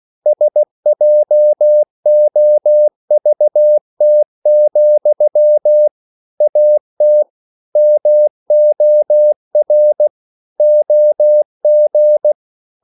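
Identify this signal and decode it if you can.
Morse code, heard as a keyed tone, about 16 wpm, 600 Hz. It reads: SJOVT, AT MOR OG
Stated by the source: Slow (QRS) Morse code at 16 wpm, from CW med Gnister.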